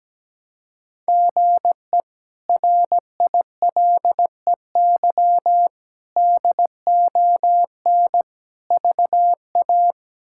Morse code, heard as a keyed tone, about 17 wpm, 700 Hz. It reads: GE RILEY DON VA
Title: K1USN SST QSO Practice - 17wpm